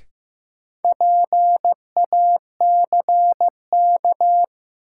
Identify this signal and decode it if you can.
Morse code, heard as a keyed tone, about 15 wpm, 700 Hz. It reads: PACK
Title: Sentences from Top 1300 Words - New Word Drill - 15wpm